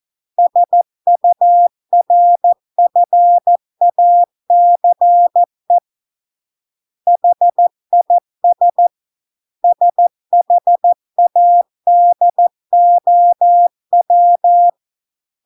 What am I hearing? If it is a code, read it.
Morse code, heard as a keyed tone, about 14 wpm, 700 Hz. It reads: SURFACE HIS SHADOW